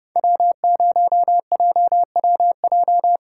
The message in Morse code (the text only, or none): W0JWJ